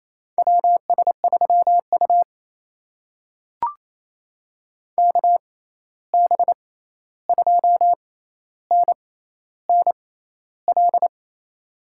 Morse code, heard as a keyed tone, about 28 wpm, spaced as for 13 wpm, 700 Hz. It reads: WS3U KB2NNL
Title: USA Call Signs - 6x Character Spacing - 28wpm